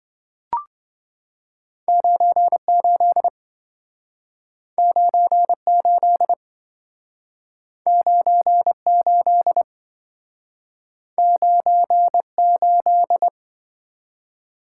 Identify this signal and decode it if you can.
Morse code, heard as a keyed tone, about 28 wpm, 700 Hz. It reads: 98 98 98 98